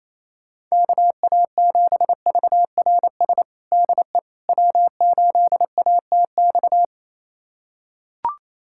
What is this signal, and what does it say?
Morse code, heard as a keyed tone, about 28 wpm, 700 Hz. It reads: KA7VRS DE W8ATX
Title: USA Callsign DE Callsign 28wpm